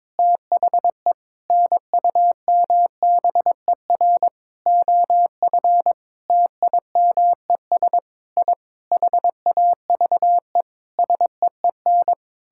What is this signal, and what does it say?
Morse code, heard as a keyed tone, about 22 wpm, 700 Hz. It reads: THE NUMBER OF TIMES I HAVE SEEN
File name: practice / 22wpm / Sentences with 8-Word Phrases - 22wpm